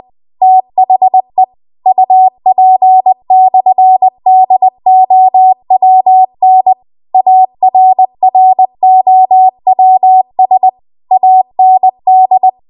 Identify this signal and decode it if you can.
Morse code, heard as a keyed tone, about 20 wpm, 750 Hz. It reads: THE UP/DOWN ARROWS AND
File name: Archive / 20 WPM / 20WPM